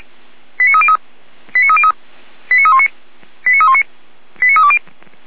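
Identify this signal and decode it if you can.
Five-tone selective call (ZVEI), heard as quick runs of five tones, tones of about 70 ms. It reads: 89303 89303 89319 89319 89320
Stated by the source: ZVEI five-tone call received by SDR.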